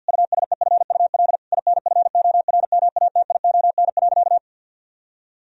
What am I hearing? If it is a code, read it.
Morse code, heard as a keyed tone, about 50 wpm, 700 Hz. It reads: USEFUL INFORMATION.
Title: Sentences from Top 1000 Words - Review Part 4 - 50wpm